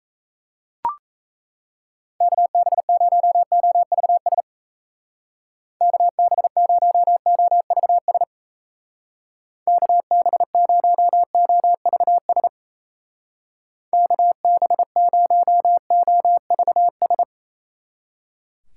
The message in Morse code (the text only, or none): KB0OVS KB0OVS KB0OVS KB0OVS